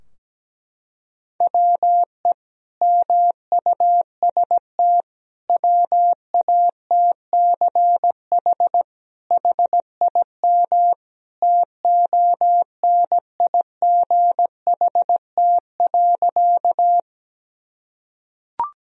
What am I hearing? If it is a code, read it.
Morse code, heard as a keyed tone, about 17 wpm, 700 Hz. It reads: WE MUST WATCH HIM TONIGHT.
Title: Sentences from Top 1000 Words - Review Part 3 - 17wpm